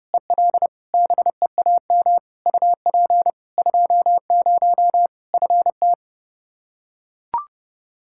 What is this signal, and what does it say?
Morse code, heard as a keyed tone, about 30 wpm, 700 Hz. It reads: EL BEAM UP 20 FT
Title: Practice QSO - Exchange 2 - Info - 30wpm